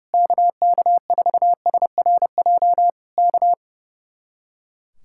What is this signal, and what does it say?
Morse code, heard as a keyed tone, about 28 wpm, 700 Hz. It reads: KK4SRJ K
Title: Speed-Racing Common QSO Phrases - 30wpm